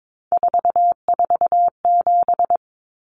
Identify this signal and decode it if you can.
Morse code, heard as a keyed tone, about 22 wpm, 700 Hz. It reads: 447
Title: Common Cut Numbers - 3-Digit 22wpm